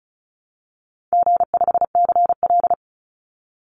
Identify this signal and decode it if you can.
Morse code, heard as a keyed tone, about 34 wpm, 700 Hz. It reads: G5CL
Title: Call Signs Repeated 2x - 35wpm